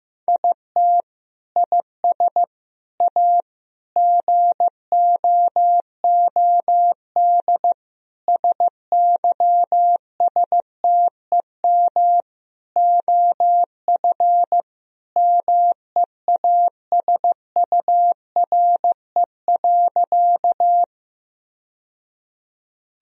Morse code, heard as a keyed tone, about 15 wpm, 700 Hz. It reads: IT IS A GOOD SYSTEM OF MEASURE.
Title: Sentences from Top 1000 Words - Review Part 3 - 15wpm